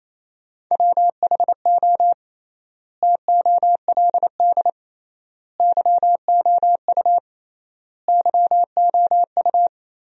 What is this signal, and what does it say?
Morse code, heard as a keyed tone, about 28 wpm, 700 Hz. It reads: WHO TOLD YOU YOU
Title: 3x Word-Spacing - Sentences from Top 400 Words - 28wpm